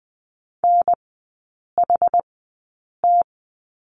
Morse code, heard as a keyed tone, about 20 wpm, spaced as for 8 wpm, 700 Hz. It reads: NHT
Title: 081 Lesson 14e - Number 2 - Three Characters 20wpm